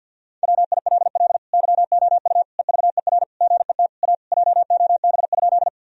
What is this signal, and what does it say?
Morse code, heard as a keyed tone, about 50 wpm, 700 Hz. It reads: WILL YOU EVER GET A JOB?
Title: Sentences from Top 1000 Words - Review Part 3 - 50wpm